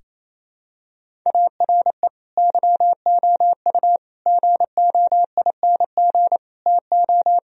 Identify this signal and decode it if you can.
Morse code, heard as a keyed tone, about 28 wpm, 700 Hz. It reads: ARE YOU GOING TO
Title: Sentences from Top 1000 Words - Review Part 1 - 28wpm